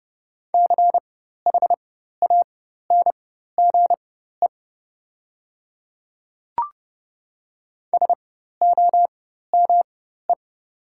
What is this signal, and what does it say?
Morse code, heard as a keyed tone, about 30 wpm, spaced as for 14 wpm, 700 Hz. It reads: CHANGE SOME
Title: 4x Character Spacing - Top 200 Words - 30wpm